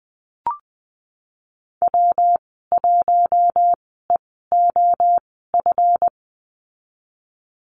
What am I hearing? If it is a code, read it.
Morse code, heard as a keyed tone, about 19 wpm, spaced as for 14 wpm, 700 Hz. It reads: W1EOF
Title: 167 Lesson 27g - Number 7 - Call Signs 20wpm